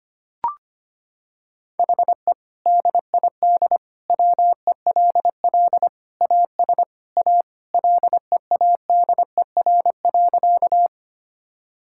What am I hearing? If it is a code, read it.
Morse code, heard as a keyed tone, about 25 wpm, 700 Hz. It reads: HE DID WELL AS A LEADER.
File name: practice / 25wpm / Sentences with 2-Word Phrases - 25wpm